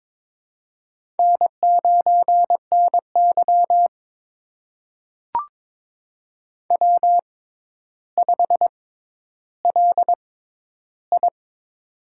Morse code, heard as a keyed tone, about 22 wpm, spaced as for 11 wpm, 700 Hz. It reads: N9NY W5LI
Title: USA Call Signs - 6x Character Spacing - 22wpm